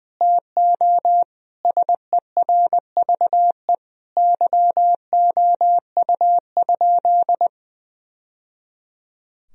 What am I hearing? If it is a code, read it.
Morse code, heard as a keyed tone, about 21 wpm, 700 Hz. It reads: TO SERVE YOU?
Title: Sentences from Top 1000 Words - Review Part 2 - 20wpm